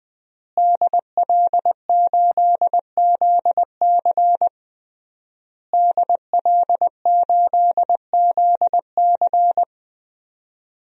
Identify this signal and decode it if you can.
Morse code, heard as a keyed tone, about 20 wpm, 700 Hz. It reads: DL8ZC DL8ZC
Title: DXCC-100 Call Signs - 5 Characters - Repeated 3x - 20wpm